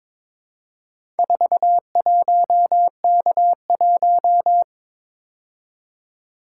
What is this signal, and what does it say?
Morse code, heard as a keyed tone, about 22 wpm, 700 Hz. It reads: 41K1
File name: practice / 22wpm / Four Letter-Number Combinations - 22wpm